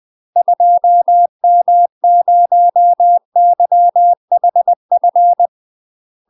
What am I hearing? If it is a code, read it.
Morse code, heard as a keyed tone, about 20 wpm, 700 Hz. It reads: 2M0YHF
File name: UK MASTER QSOs TRAINING/20WPM